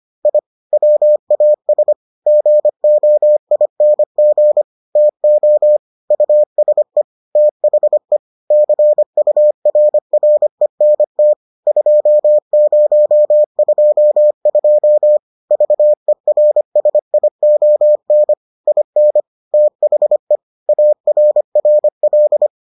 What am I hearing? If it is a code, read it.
Morse code, heard as a keyed tone, about 25 wpm, 600 Hz. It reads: I WAS GOING TO USE THE CURRENT 2022 VERSION IN THE ARRL